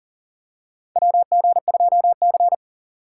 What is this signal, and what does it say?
Morse code, heard as a keyed tone, about 39 wpm, 700 Hz. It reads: WG2C
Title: USA Call Signs - 4 Characters - 2x Character Spacing - 40wpm